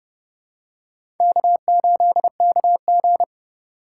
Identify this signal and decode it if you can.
Morse code, heard as a keyed tone, about 32 wpm, 700 Hz. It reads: K8KG
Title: USA Call Signs - 4 Characters - 3x Character Spacing - 30wpm